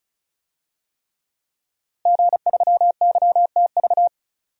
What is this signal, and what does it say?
Morse code, heard as a keyed tone, about 34 wpm, 700 Hz. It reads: G3YTV
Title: DXCC-100 Call Signs - 5 Characters - Repeated 3x - 35wpm